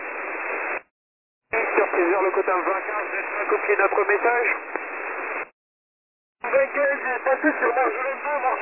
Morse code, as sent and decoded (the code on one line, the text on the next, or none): none